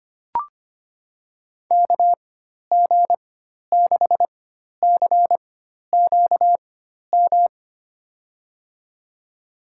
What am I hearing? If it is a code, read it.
Morse code, heard as a keyed tone, about 25 wpm, spaced as for 12 wpm, 700 Hz. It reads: KG6CQM
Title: USA Call Signs - 6 Characters - 4x Character Spacing - 25wpm